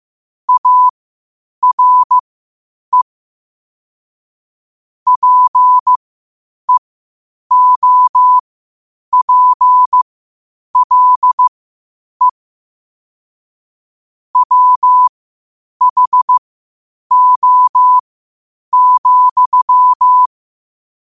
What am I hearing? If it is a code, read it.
Morse code, heard as a keyed tone, about 15 wpm, spaced as for 8 wpm, 1000 Hz. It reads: ARE PEOPLE WHO,
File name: qotdfiles / 8wpm